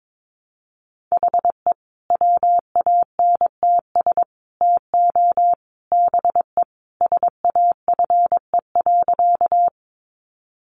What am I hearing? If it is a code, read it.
Morse code, heard as a keyed tone, about 22 wpm, 700 Hz. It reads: HE WANTS TO BE SAFE.
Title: Additional Easy Phrases 22wpm